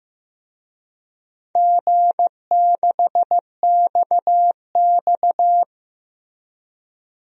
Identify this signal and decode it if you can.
Morse code, heard as a keyed tone, about 15 wpm, 700 Hz. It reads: G6XX
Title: DXCC-100 Call Signs - 4 Characters - Repeated 2x - 15wpm